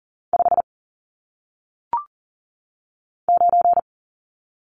Morse code, heard as a keyed tone, about 39 wpm, 700 Hz. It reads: 5 9
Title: Numbers - 1-Digit 40wpm